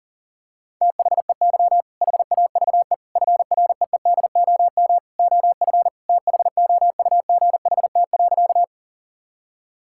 Morse code, heard as a keyed tone, about 40 wpm, 700 Hz. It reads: THEY HAVE FREEDOM OF THOUGHT.
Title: Sentences from Top 1000 Words - Review Part 5 - 40wpm